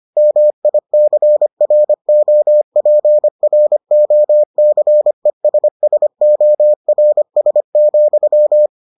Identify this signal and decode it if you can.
Morse code, heard as a keyed tone, about 25 wpm, 600 Hz. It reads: MICROPROCESSORS,